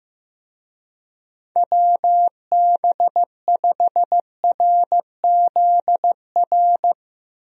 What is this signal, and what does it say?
Morse code, heard as a keyed tone, about 15 wpm, 700 Hz. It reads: WB5RZR
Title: USA Call Signs - 6 Characters - Repeated 3x - 15wpm